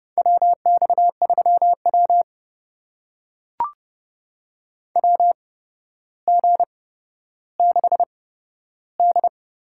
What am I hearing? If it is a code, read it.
Morse code, heard as a keyed tone, about 32 wpm, spaced as for 13 wpm, 700 Hz. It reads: WX3W WG6D
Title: USA Call Signs - 4 Characters - 8x Character Spacing - 30wpm